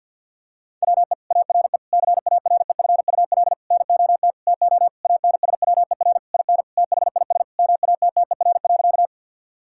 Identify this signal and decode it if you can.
Morse code, heard as a keyed tone, about 50 wpm, 700 Hz. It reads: WE ARE CAREFUL NOT TO ANSWER IN THIS MATTER.
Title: Sentences from Top 1600 Words 50wpm